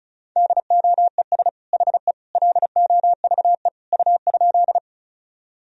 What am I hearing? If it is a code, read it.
Morse code, heard as a keyed tone, about 36 wpm, 700 Hz. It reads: DOES HE LOVE U?
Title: Sentences from Top 400 Words and Abbreviations 35wpm